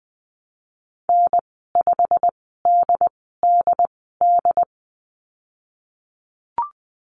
Morse code, heard as a keyed tone, about 20 wpm, spaced as for 15 wpm, 700 Hz. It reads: N5DDD
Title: 154 Lesson 25g - Letter p - Call Signs 20wpm